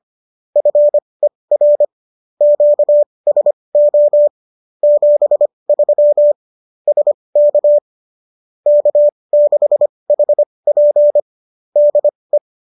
Morse code, heard as a keyed tone, about 25 wpm, spaced as for 20 wpm, 590 Hz. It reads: FER QSO 73 SK K6HP DE
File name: qso205 20